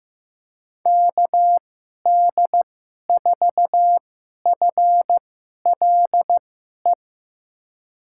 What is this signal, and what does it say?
Morse code, heard as a keyed tone, about 15 wpm, spaced as for 11 wpm, 700 Hz. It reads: KD4FLE
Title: USA Call Signs - 6 Characters - 2x Character Spacing - 15wpm